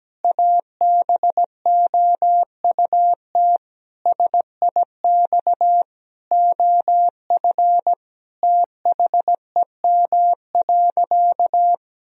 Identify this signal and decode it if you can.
Morse code, heard as a keyed tone, about 17 wpm, 700 Hz. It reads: ABOUT SIX OF THEM.
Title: Sentences from Top 1000 Words - Review Part 2 - 17wpm